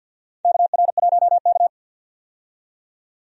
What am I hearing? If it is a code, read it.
Morse code, heard as a keyed tone, about 51 wpm, 700 Hz. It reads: KR1K
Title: USA Call Signs - 4 Characters - 50wpm